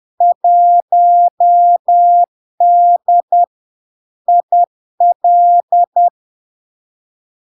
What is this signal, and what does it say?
Morse code, heard as a keyed tone, about 10 wpm, 700 Hz. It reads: 1D IL
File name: LARK-FD-2022-Log-in-CW-10wpm-700-Hz